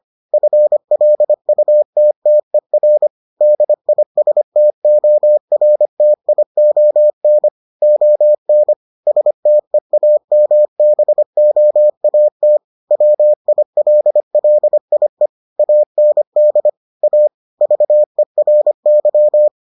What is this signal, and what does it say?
Morse code, heard as a keyed tone, about 25 wpm, 600 Hz. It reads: FLUTTER DISTORTION ON STEAMBOAT WILLIE AND A VERY